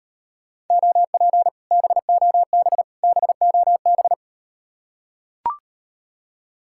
Morse code, heard as a keyed tone, about 39 wpm, 700 Hz. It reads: OP BOB BOB